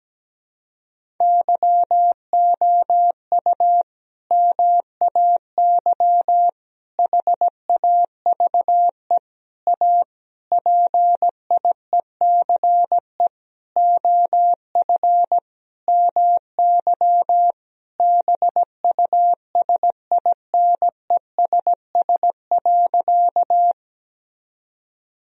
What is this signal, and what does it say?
Morse code, heard as a keyed tone, about 17 wpm, 700 Hz. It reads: YOU MAY HAVE A PIECE OF MY BUSINESS.
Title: Sentences from Top 1000 Words - Review Part 2 - 17wpm